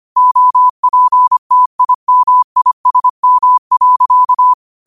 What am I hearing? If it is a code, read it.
Morse code, heard as a keyed tone, about 25 wpm, 1000 Hz. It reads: OPTIMISM.
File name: qotdfiles / 25wpm